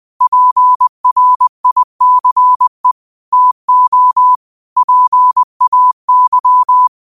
Morse code, heard as a keyed tone, about 20 wpm, 1000 Hz. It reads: PRICE TO PAY
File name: 20wpm